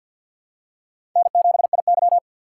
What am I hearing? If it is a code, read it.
Morse code, heard as a keyed tone, about 52 wpm, 690 Hz. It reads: N7IY
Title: USA Call Signs - 2x Character Spacing - 50wpm